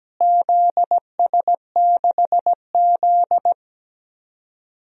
Morse code, heard as a keyed tone, about 17 wpm, 700 Hz. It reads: ZS6Z
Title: DXCC-100 Call Signs - 4 Characters - Repeated 3x - 17wpm